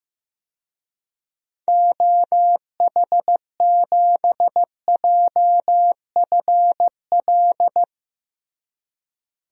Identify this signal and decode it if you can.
Morse code, heard as a keyed tone, about 15 wpm, 700 Hz. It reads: OH7JFL
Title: DXCC-100 Call Signs - 6 Characters - Repeated 3x - 15wpm